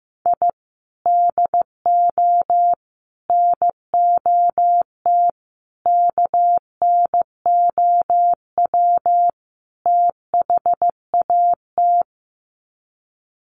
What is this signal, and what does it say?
Morse code, heard as a keyed tone, about 15 wpm, 700 Hz. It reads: I DO NOT KNOW THAT
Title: Sets of 5 Words 15wpm